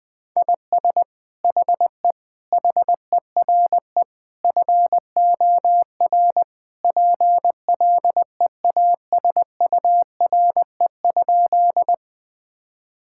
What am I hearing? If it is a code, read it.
Morse code, heard as a keyed tone, about 20 wpm, 700 Hz. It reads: IS HE HERE FOR PLEASURE?
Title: Sentences from Top 1600 Words 20wpm